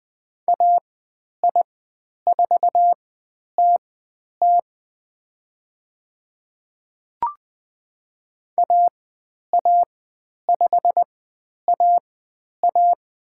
Morse code, heard as a keyed tone, about 20 wpm, spaced as for 10 wpm, 700 Hz. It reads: AI4TT AA5AA
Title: 102 Lesson 17g - Letter u - Call Signs 20wpm